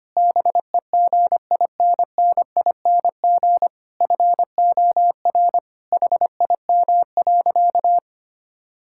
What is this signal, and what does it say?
Morse code, heard as a keyed tone, about 25 wpm, 700 Hz. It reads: BEGINNING FOR HIM.
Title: Sentences from Top 1000 Words - Review Part 5 - 25wpm